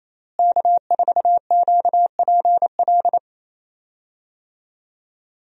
Morse code, heard as a keyed tone, about 28 wpm, 700 Hz. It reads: K4QPL
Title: USA Call Signs - 5 Characters - Repeated 2x - 28wpm